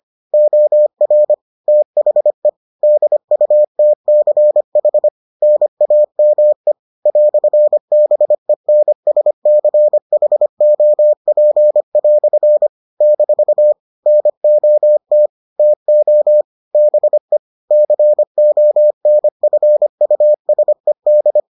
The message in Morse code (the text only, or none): OR THE DUTCH NAME "BENSCHOP" - NOT TO BE CONFUSED